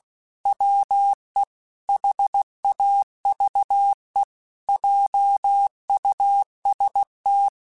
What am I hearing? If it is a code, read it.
Morse code, heard as a keyed tone, about 16 wpm, 800 Hz. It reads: WE HAVE JUST
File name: Technical Subjects / Technical Subjects-mp3s / OR Gate @15